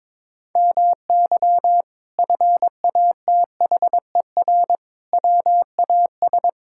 Morse code, heard as a keyed tone, about 22 wpm, 700 Hz. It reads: MY FATHER WAS
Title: Sentences from Top 1000 Words - Review Part 6 - 22wpm